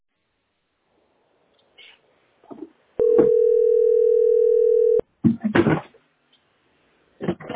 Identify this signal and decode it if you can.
Ringback tone (North America), 440 + 480 Hz.
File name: 5d5cfd49-6a88-45ca-9b94